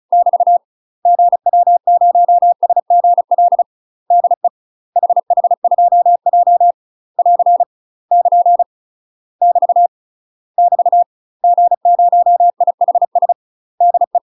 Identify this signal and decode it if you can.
Morse code, heard as a keyed tone, about 35 wpm, 700 Hz. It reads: <BT> GW0SGL DE HH2J <AR> <KN> <BT> <BT> G0IHS DE
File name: UK MASTER QSOs TRAINING/35WPM